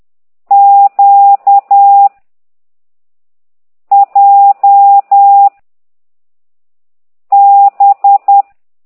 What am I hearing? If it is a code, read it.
Morse code, heard as a keyed tone, about 10 wpm, 800 Hz. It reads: Q J B
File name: ct2idw / exerc15